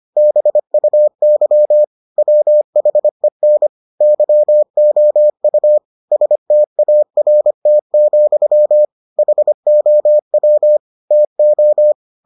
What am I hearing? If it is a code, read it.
Morse code, heard as a keyed tone, about 25 wpm, 600 Hz. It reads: BUY WHEN YOU START, HOW TO